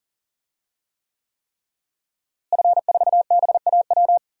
Morse code, heard as a keyed tone, about 40 wpm, 700 Hz. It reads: F4BAW
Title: DXCC-100 Call Signs - 5 Characters - Repeated 3x - 40wpm